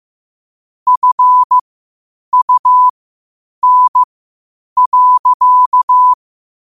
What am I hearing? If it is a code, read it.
Morse code, heard as a keyed tone, about 15 wpm, spaced as for 8 wpm, 1000 Hz. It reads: FUN.
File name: qotdfiles / 8wpm